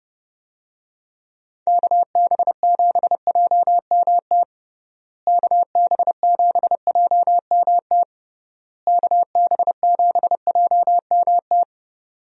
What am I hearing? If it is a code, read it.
Morse code, heard as a keyed tone, about 29 wpm, 700 Hz. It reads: KB7JMT KB7JMT KB7JMT
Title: USA Call Signs - 6 Characters - Repeated 3x - 30wpm